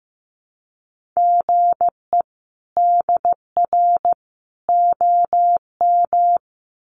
Morse code, heard as a keyed tone, about 15 wpm, 700 Hz. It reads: GE DR OM
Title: Common QSO Phrases 15wpm